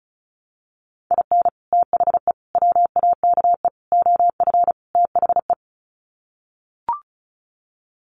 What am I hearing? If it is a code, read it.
Morse code, heard as a keyed tone, about 34 wpm, 700 Hz. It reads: IN THE WAKE OF THE
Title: Sets of 5 Words 35wpm